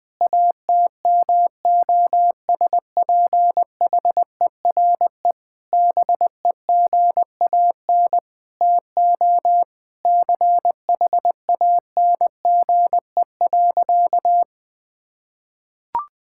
Morse code, heard as a keyed tone, about 21 wpm, 700 Hz. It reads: ATMOSPHERE BEGAN TO CHANGE.
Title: Sentences from Top 1600 Words 20wpm